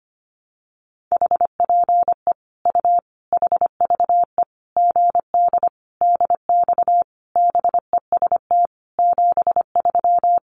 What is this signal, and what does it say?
Morse code, heard as a keyed tone, about 25 wpm, 700 Hz. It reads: HPE U HVE GD DX BEST 73
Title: Common QSO Phrases 25wpm